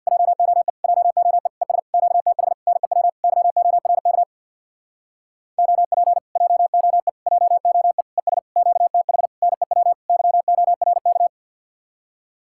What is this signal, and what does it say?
Morse code, heard as a keyed tone, about 67 wpm, 700 Hz. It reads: JOE JOE ES QTH NEW YORK OP JOE JOE ES QTH NEW YORK